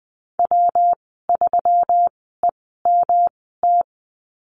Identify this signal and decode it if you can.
Morse code, heard as a keyed tone, about 20 wpm, spaced as for 15 wpm, 700 Hz. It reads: W3EMT
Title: 122 Lesson 20g - Number 3 - Call Signs 20wpm